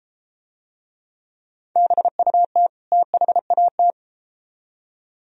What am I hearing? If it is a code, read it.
Morse code, heard as a keyed tone, about 33 wpm, 700 Hz. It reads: BUT THAT